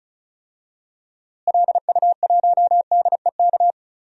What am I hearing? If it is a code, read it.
Morse code, heard as a keyed tone, about 36 wpm, 700 Hz. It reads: LU1DEK
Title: DXCC-100 Call Signs - 6 Characters - Repeated 3x - 35wpm